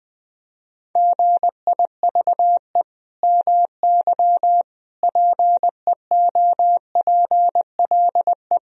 Morse code, heard as a keyed tone, about 21 wpm, 700 Hz. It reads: GIVE MY PEOPLE
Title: Sentences from Top 1000 Words - Review Part 1 - 20wpm